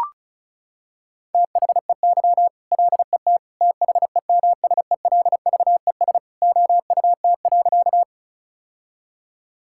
Morse code, heard as a keyed tone, about 34 wpm, 700 Hz. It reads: THEY LET THEMSELVES OUT.
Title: Sentences from Top 1000 Words - Review Part 5 - 35wpm